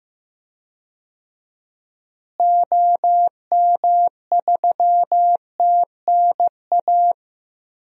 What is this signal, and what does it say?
Morse code, heard as a keyed tone, about 15 wpm, 700 Hz. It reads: OM3TNA
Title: DXCC-100 Call Signs - 6 Characters - Repeated 3x - 15wpm